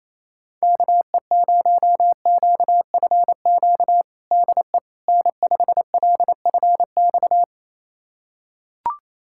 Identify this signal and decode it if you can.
Morse code, heard as a keyed tone, about 28 wpm, 700 Hz. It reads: KE0QFQ DE N5LFX
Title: USA Callsign DE Callsign 28wpm